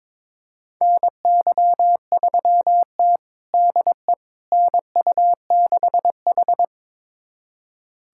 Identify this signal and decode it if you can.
Morse code, heard as a keyed tone, about 22 wpm, 700 Hz. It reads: NY3T DE NU6H
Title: USA Callsign DE Callsign - 4 Characters - 22wpm